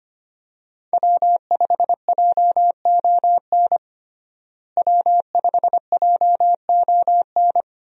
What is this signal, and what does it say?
Morse code, heard as a keyed tone, about 25 wpm, 700 Hz. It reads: W5JON W5JON
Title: USA Call Signs - 5 Characters - Repeated 2x - 25wpm